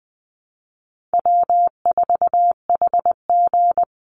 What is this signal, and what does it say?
Morse code, heard as a keyed tone, about 20 wpm, 700 Hz. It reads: W4HG
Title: Easy Call Signs Repeated 2x - 20wpm